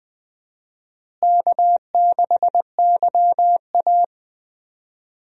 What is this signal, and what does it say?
Morse code, heard as a keyed tone, about 20 wpm, 700 Hz. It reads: K6YA
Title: USA Call Signs - 4 Characters - Repeated 2x - 20wpm